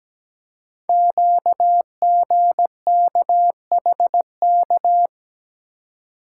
Morse code, heard as a keyed tone, about 17 wpm, 700 Hz. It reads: QGKHK
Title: Code Groups - 5 Letters - 17wpm